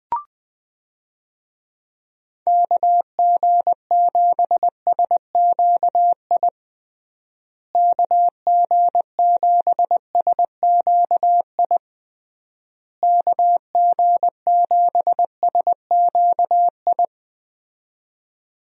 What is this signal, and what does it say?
Morse code, heard as a keyed tone, about 20 wpm, 700 Hz. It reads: KG7SQI KG7SQI KG7SQI
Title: USA Call Signs - 6 Characters - Repeated 3x - 20wpm